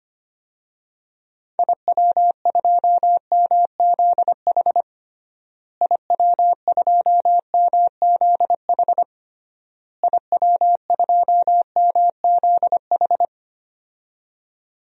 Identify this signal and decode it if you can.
Morse code, heard as a keyed tone, about 25 wpm, 700 Hz. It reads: IW2MZH IW2MZH IW2MZH
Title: DXCC-100 Call Signs - 6 Characters - Repeated 3x - 25wpm